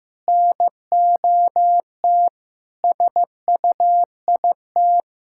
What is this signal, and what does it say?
Morse code, heard as a keyed tone, about 15 wpm, 700 Hz. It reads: NOT SUIT
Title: Sentences from Top 1600 Words 15wpm